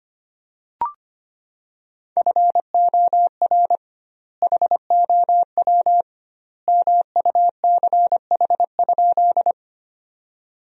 Morse code, heard as a keyed tone, about 25 wpm, 700 Hz. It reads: FOR HOW MUCH?
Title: 2x Word-Spacing - Sentences from Top 200 Words - 25wpm